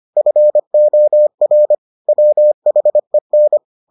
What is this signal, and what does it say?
Morse code, heard as a keyed tone, about 25 wpm, 600 Hz. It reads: FOR WHEN